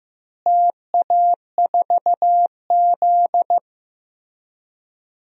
Morse code, heard as a keyed tone, about 15 wpm, 700 Hz. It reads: TA4Z
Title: DXCC-100 Call Signs - 4 Characters - Repeated 2x - 15wpm